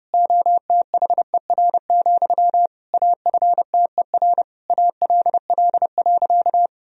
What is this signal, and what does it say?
Morse code, heard as a keyed tone, about 30 wpm, 700 Hz. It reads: OTHER, AFTER ALL.
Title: Sentences from Top 1000 Words - Review Part 1 - 30wpm